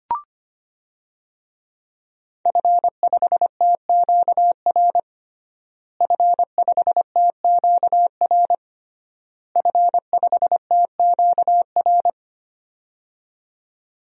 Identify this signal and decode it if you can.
Morse code, heard as a keyed tone, about 25 wpm, 700 Hz. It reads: F5TQR F5TQR F5TQR
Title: DXCC-100 Call Signs - 5 Characters - Repeated 3x - 25wpm